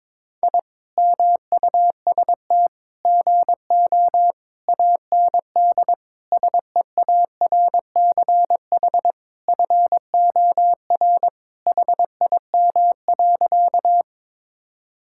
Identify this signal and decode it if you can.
Morse code, heard as a keyed tone, about 22 wpm, 700 Hz. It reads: I MUST GO AND SEARCH FOR HIM.
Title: Sentences from Top 1000 Words - Review Part 5 - 22wpm